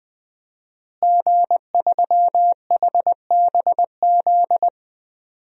Morse code, heard as a keyed tone, about 21 wpm, 700 Hz. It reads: G3HBZ
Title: DXCC-100 Call Signs - 5 Characters - Repeated 2x - 20wpm